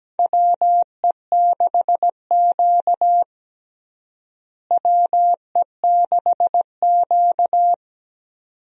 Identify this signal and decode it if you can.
Morse code, heard as a keyed tone, about 17 wpm, 700 Hz. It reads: WE6Q WE6Q
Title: USA Call Signs - 4 Characters - Repeated 2x - 17wpm